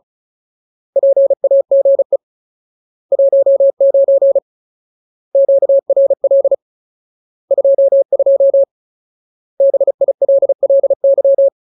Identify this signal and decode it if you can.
Morse code, heard as a keyed tone, about 34 wpm, 550 Hz. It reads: PAGE 19 QRL 22 BILLY
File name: Ultimate Word List / 35WPM Ultimate